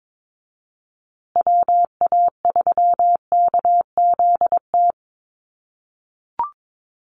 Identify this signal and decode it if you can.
Morse code, heard as a keyed tone, about 22 wpm, 700 Hz. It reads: WA3KZT
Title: USA Call Signs Repeated 3x 22wpm